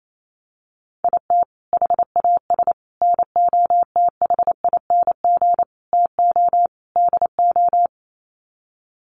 Morse code, heard as a keyed tone, about 28 wpm, 700 Hz. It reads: IT HAS NOTHING TO DO
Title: Sets of 5 Words Encore 28wpm